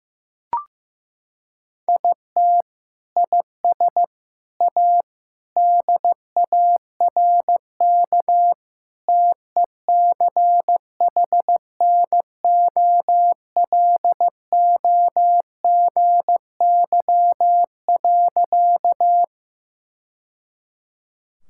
Morse code, heard as a keyed tone, about 15 wpm, 700 Hz. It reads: IT IS A DARK TECHNOLOGY.
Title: Sentences from Top 1000 Words - Review Part 5 - 15wpm